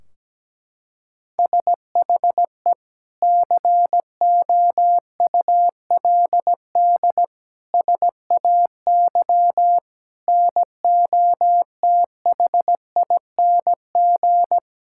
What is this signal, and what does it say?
Morse code, heard as a keyed tone, about 17 wpm, 700 Hz. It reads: SHE COULD SAY NOTHING
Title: Sentences from Top 1600 Words 17wpm